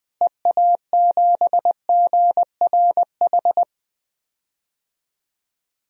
Morse code, heard as a keyed tone, about 20 wpm, 700 Hz. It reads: EA7GRH
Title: DXCC-100 Call Signs - 6 Characters - Repeated 2x - 20wpm